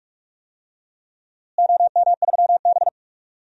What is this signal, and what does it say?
Morse code, heard as a keyed tone, about 45 wpm, 700 Hz. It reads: OM3B
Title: DXCC-100 Call Signs - 4 Characters - Repeated 3x - 45wpm